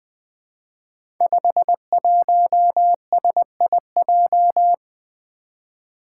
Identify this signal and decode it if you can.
Morse code, heard as a keyed tone, about 20 wpm, 700 Hz. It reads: 51SIJ
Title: Five Challenging Letter-Number Combinations 20wpm